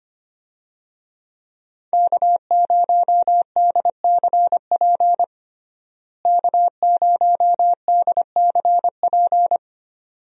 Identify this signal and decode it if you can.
Morse code, heard as a keyed tone, about 25 wpm, 700 Hz. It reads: K0DCP K0DCP
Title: USA Call Signs - 5 Characters - Repeated 2x - 25wpm